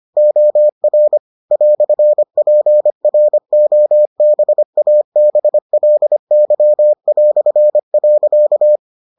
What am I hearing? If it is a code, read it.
Morse code, heard as a keyed tone, about 25 wpm, 600 Hz. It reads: OR "PROBABLY".